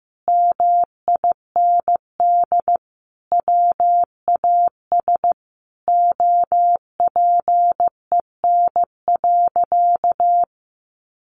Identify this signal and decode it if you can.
Morse code, heard as a keyed tone, about 15 wpm, 700 Hz. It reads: MIND WAS OPEN.
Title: Sentences from Top 1200 Words 15wpm